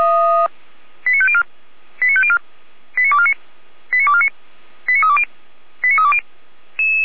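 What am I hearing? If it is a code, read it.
Five-tone selective call (ZVEI), heard as quick runs of five tones, tones of about 70 ms. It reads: 89504 89504 89249 89249 89320 89320
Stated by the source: Five-tone ZVEI selective call received on an SDR.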